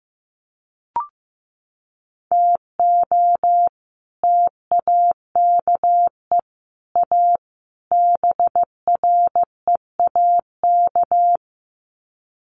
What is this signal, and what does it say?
Morse code, heard as a keyed tone, about 15 wpm, 700 Hz. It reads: TO TAKE A BREAK
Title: Sets of 4 Words 15wpm